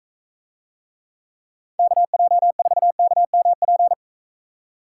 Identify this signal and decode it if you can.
Morse code, heard as a keyed tone, about 45 wpm, 700 Hz. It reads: KJ4KMP